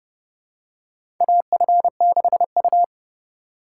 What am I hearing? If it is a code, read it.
Morse code, heard as a keyed tone, about 30 wpm, 700 Hz. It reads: AF6U
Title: USA Call Signs - 4 Characters - 2x Character Spacing - 30wpm